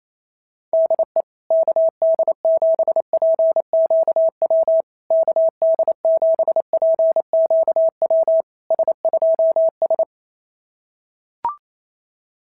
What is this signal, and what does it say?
Morse code, heard as a keyed tone, about 28 wpm, 650 Hz. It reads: DE KD7PQW KD7PQW S2S
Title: SOTA QSO Practice 28wpm